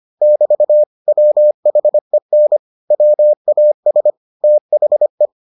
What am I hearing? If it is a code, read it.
Morse code, heard as a keyed tone, about 25 wpm, 600 Hz. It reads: <BT> WHEN WAS THE